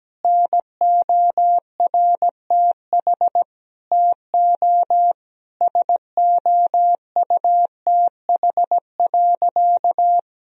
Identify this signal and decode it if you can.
Morse code, heard as a keyed tone, about 17 wpm, 700 Hz. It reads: NORTH TO SOUTH.
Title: Sentences from Top 1000 Words - Review Part 2 - 17wpm